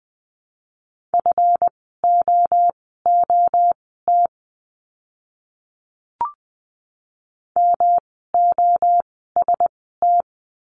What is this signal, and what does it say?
Morse code, heard as a keyed tone, about 20 wpm, spaced as for 15 wpm, 700 Hz. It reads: FOOT MOST
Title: 139 Lesson 23f - Letter f - Words 20wpm